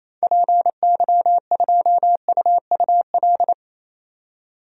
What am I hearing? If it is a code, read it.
Morse code, heard as a keyed tone, about 28 wpm, 700 Hz. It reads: PY2UUL
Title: DXCC-100 Call Signs - 6 Characters - 28wpm